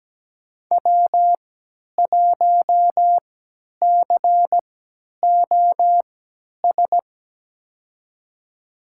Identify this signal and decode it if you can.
Morse code, heard as a keyed tone, about 17 wpm, spaced as for 10 wpm, 700 Hz. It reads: W1COS
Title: USA Call Signs - 5 Characters - 3x Character Spacing - 17wpm